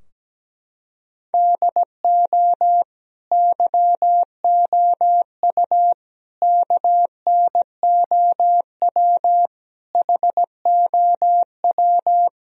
Morse code, heard as a keyed tone, about 17 wpm, 700 Hz. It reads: DO YOU KNOW HOW
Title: Sentences from Top 1000 Words - Review Part 3 - 17wpm